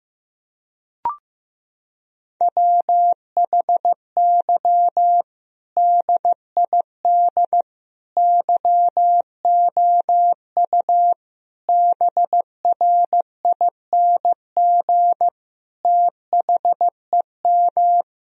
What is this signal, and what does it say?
Morse code, heard as a keyed tone, about 15 wpm, 700 Hz. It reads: WHY DID YOU BRING THEM
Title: Sentences from Top 1000 Words - Review Part 3 - 15wpm